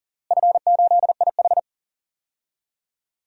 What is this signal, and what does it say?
Morse code, heard as a keyed tone, about 40 wpm, 700 Hz. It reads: F8IH
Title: DXCC-100 Call Signs - 4 Characters - Repeated 3x - 40wpm